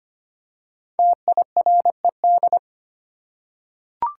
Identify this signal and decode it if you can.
Morse code, heard as a keyed tone, about 25 wpm, 700 Hz. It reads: TIRED
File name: 5 Letter Words - First 3 Letters Given - 25wpm